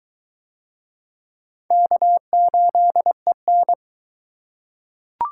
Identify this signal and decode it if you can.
Morse code, heard as a keyed tone, about 21 wpm, 700 Hz. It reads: K8EN